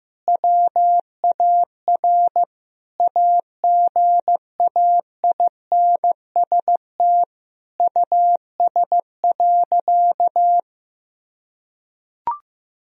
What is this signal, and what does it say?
Morse code, heard as a keyed tone, about 16 wpm, 700 Hz. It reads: WAR AGAINST US.
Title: Sentences from Top 1000 Words - Review Part 2 - 15wpm